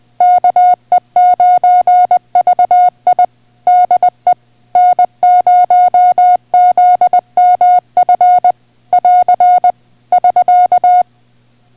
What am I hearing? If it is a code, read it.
Morse code, heard as a keyed tone, about 20 wpm, 720 Hz. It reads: KE9VI DE N0ZMF <AR> <SK>